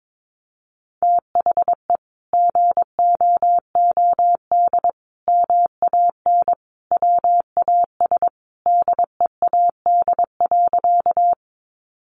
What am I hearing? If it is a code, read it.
Morse code, heard as a keyed tone, about 22 wpm, 700 Hz. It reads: THE GOOD MAN WAS DEAD.
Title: Sentences from Top 1500 Words 22wpm